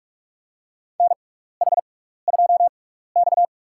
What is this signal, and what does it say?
Morse code, heard as a keyed tone, about 46 wpm, spaced as for 16 wpm, 700 Hz. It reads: NH2X
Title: USA Call Signs - 4 Characters - 6x Character Spacing - 45wpm